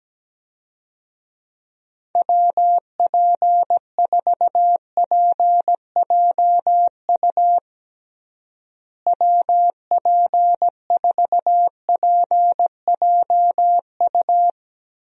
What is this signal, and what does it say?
Morse code, heard as a keyed tone, about 17 wpm, 700 Hz. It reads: WP4PJU WP4PJU
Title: USA Call Signs - 6 Characters - Repeated 2x - 17wpm